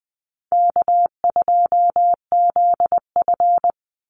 Morse code, heard as a keyed tone, about 20 wpm, 700 Hz. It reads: K2ZF
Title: Call Signs - 20wpm